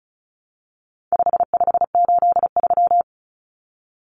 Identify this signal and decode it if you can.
Morse code, heard as a keyed tone, about 34 wpm, 700 Hz. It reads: 5583
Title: Common Cut Numbers - 4-Digit 35wpm